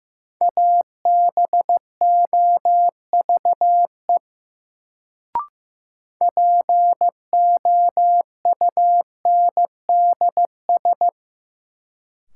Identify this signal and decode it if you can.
Morse code, heard as a keyed tone, about 15 wpm, 700 Hz. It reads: ABOVE POUNDS
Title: Sentences from Top 1000 Words - Review Part 2 - New Word Drill - 15wpm